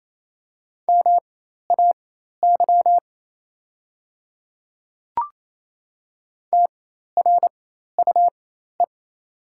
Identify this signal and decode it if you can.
Morse code, heard as a keyed tone, about 28 wpm, spaced as for 13 wpm, 700 Hz. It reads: MAY TRUE
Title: 4x Character Spacing - Top 500 Words - 28wpm